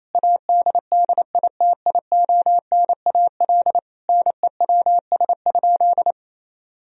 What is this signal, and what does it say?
Morse code, heard as a keyed tone, about 28 wpm, 700 Hz. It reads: ADDITIONAL NEWS?
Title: Sentences from Top 1000 Words - Review Part 4 - 28wpm